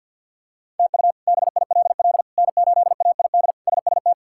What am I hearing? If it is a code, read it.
Morse code, heard as a keyed tone, about 52 wpm, 700 Hz. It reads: TU BILL N8AID SST
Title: K1USN SST QSO Practice - 50wpm